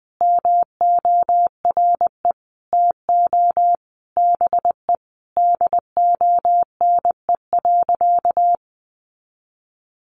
Morse code, heard as a keyed tone, about 20 wpm, 700 Hz. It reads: MORE TO BE DONE.
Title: Sentences from Top 500 Words 20wpm